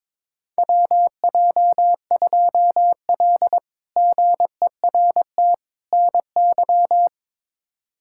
Morse code, heard as a keyed tone, about 22 wpm, 700 Hz. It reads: WJ2L GERT NY
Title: K1USN SST QSO Practice - 22wpm